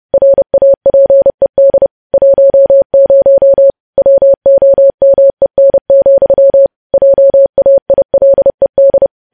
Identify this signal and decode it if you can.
Morse code, heard as a keyed tone, about 30 wpm, 550 Hz. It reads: RAPED 10 WOMEN, JAILED
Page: news headlines in morse code at 30 wpm